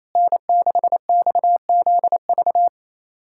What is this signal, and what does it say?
Morse code, heard as a keyed tone, about 28 wpm, 700 Hz. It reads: N6XZV
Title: USA Callsign DE Callsign - 5 Characters - 28wpm